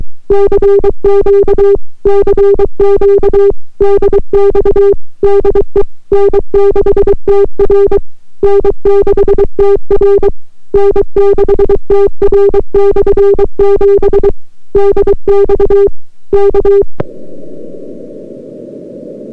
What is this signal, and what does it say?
Morse code, heard as a keyed tone, about 23 wpm, 400 Hz. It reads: CQ CQ DX DE N6TR N6TR N6TR/7 DX K